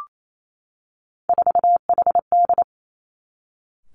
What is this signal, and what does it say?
Morse code, heard as a keyed tone, about 28 wpm, 700 Hz. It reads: 4HD
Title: Three Challenging Letter-Number Combinations 28wpm